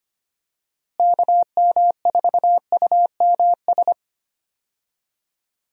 Morse code, heard as a keyed tone, about 25 wpm, 700 Hz. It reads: KM4UMS
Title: USA Call Signs - 6 Characters - Repeated 2x - 25wpm